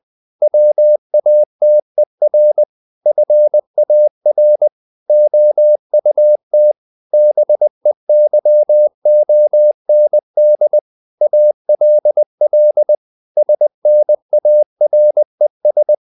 Morse code, heard as a keyed tone, about 20 wpm, 600 Hz. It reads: WATER FAR OUT BEYOND ALL SNARES
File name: the old man and the sea / the old man and the sea part 1 20wpm